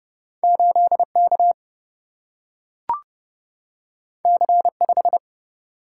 Morse code, heard as a keyed tone, about 30 wpm, 700 Hz. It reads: 8K C5
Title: Single-Letter Single-Number 30wpm